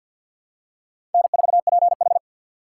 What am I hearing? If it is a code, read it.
Morse code, heard as a keyed tone, about 53 wpm, 700 Hz. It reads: N4PH
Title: USA Call Signs - 4 Characters - 6x Character Spacing - 50wpm